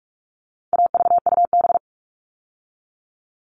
Morse code, heard as a keyed tone, about 44 wpm, 700 Hz. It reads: AVUB
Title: All Cut Numbers - 4-Digit 45wpm